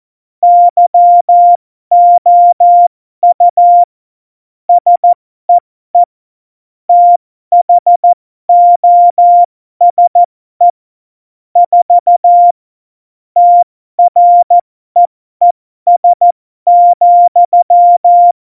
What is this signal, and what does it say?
Morse code, heard as a keyed tone, about 14 wpm, spaced as for 12 wpm, 700 Hz. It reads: YOU SEE THOSE 4 TREES,
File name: Morsekurs / Witze